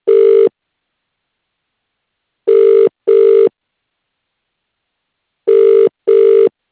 Ringback tone (UK), 400 + 450 Hz.